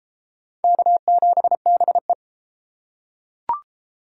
Morse code, heard as a keyed tone, about 33 wpm, 700 Hz. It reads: K7BE